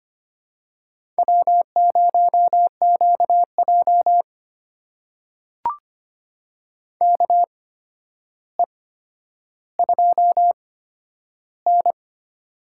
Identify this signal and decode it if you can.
Morse code, heard as a keyed tone, about 25 wpm, spaced as for 11 wpm, 700 Hz. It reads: W0QJ KE2N
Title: USA Call Signs - 4 Characters - 8x Character Spacing - 25wpm